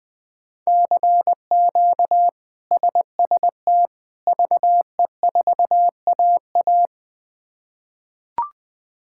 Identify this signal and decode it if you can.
Morse code, heard as a keyed tone, about 21 wpm, 700 Hz. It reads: CQ SST VE4AA
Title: K1USN SST QSO Practice - 20wpm